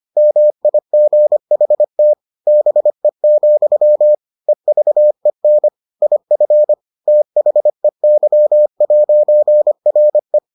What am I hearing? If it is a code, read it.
Morse code, heard as a keyed tone, about 25 wpm, 600 Hz. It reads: MIGHT BE, EVEN IF THEY'RE